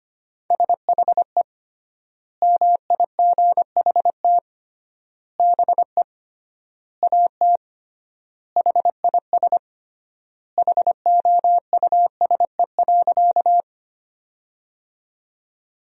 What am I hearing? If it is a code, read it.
Morse code, heard as a keyed tone, about 25 wpm, 700 Hz. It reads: SHE MIGHT BE AT HIS HOUSE.
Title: 3x Word-Spacing - Sentences from Top 300 Words - 25wpm